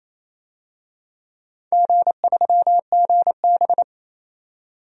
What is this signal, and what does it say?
Morse code, heard as a keyed tone, about 28 wpm, 700 Hz. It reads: G3GB
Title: DXCC-100 Call Signs - 4 Characters - 28wpm